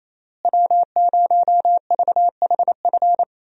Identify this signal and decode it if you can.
Morse code, heard as a keyed tone, about 28 wpm, 700 Hz. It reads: W0VHF
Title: USA Call Signs - 5 Characters - Repeated 3x - 28wpm